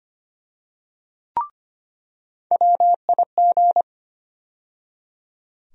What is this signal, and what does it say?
Morse code, heard as a keyed tone, about 25 wpm, 700 Hz. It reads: WIG
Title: USA Call Sign Postfixes - 3 Letters 25wpm